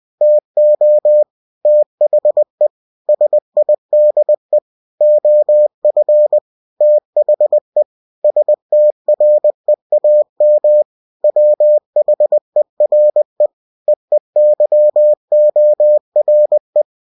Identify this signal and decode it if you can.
Morse code, heard as a keyed tone, about 20 wpm, 600 Hz. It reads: TO THE SIDE OF THE STREAM WHERE EEYORE
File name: cw-books / winnie-the-pooh / winnie-20wpm-page-0053